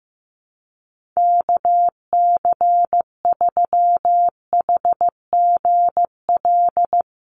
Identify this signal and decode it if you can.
Morse code, heard as a keyed tone, about 15 wpm, 700 Hz. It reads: KC3HGL
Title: USA Call Signs Repeated 2x 15wpm